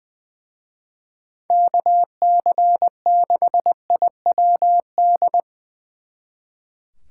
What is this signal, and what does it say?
Morse code, heard as a keyed tone, about 23 wpm, 700 Hz. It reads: KC6IWD
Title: Speed Racing - USA 2x3 Call Signs 20wpm